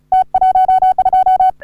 Morse code, heard as a keyed tone, about 36 wpm, 750 Hz. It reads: T12
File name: CW / HA3OU RHOBCW REC